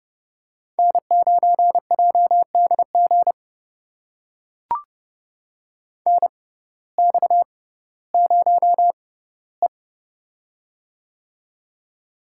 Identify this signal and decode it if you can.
Morse code, heard as a keyed tone, about 30 wpm, spaced as for 17 wpm, 700 Hz. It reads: N9JDG NX0E
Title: USA Call Signs - 6x Character Spacing - 30wpm